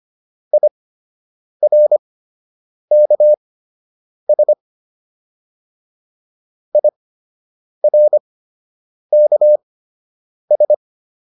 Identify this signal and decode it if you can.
Morse code, heard as a keyed tone, about 25 wpm, spaced as for 8 wpm, 600 Hz. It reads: IRKS IRKS